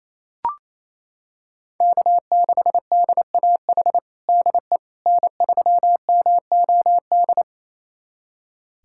Morse code, heard as a keyed tone, about 28 wpm, 700 Hz. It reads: K6DAH DE N3MOD
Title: USA Callsign DE Callsign - 5 Characters - 28wpm